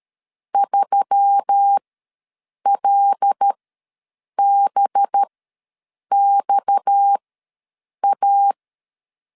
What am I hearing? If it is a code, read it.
Morse code, heard as a keyed tone, about 13 wpm, spaced as for 7 wpm, 800 Hz. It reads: 3LBXA
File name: LEZIONI / 7 GF9Z / 7 GF ws 12wpm 3140